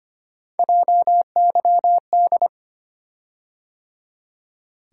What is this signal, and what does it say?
Morse code, heard as a keyed tone, about 25 wpm, 700 Hz. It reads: JYD
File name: Three Letter-Number Combinations - 25wpm